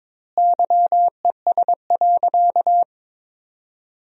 Morse code, heard as a keyed tone, about 22 wpm, 700 Hz. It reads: YES.
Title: 2x Word-Spacing - Sentences from Top 500 Words - 22wpm